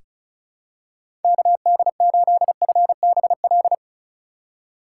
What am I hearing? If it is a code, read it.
Morse code, heard as a keyed tone, about 36 wpm, 700 Hz. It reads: KD8FBL
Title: USA Call Signs - 6 Characters - 2x Character Spacing - 35wpm